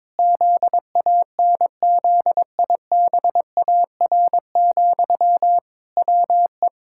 Morse code, heard as a keyed tone, about 21 wpm, 700 Hz. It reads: ZANZIBAR, WE
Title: Pangram Challenge v2 - 22wpm